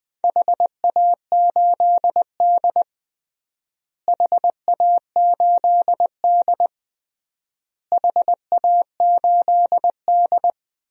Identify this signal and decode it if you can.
Morse code, heard as a keyed tone, about 20 wpm, 700 Hz. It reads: HA8D HA8D HA8D
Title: DXCC-100 Call Signs - 4 Characters - Repeated 3x - 20wpm